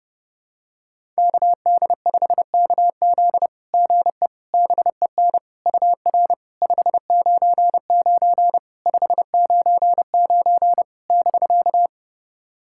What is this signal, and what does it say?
Morse code, heard as a keyed tone, about 30 wpm, 690 Hz. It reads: KD5KZ GE BEN UR 599 599 <BK>
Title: SOTA QSO Practice 30wpm